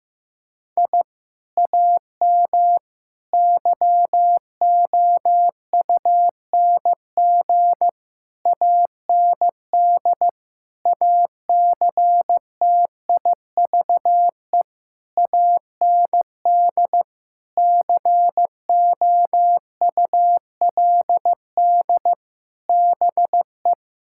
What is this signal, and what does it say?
Morse code, heard as a keyed tone, about 15 wpm, 700 Hz. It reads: I AM YOUNG AND ACTIVE AND COULD BE
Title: Sentences from Top 1000 Words - Review Part 6 - 15wpm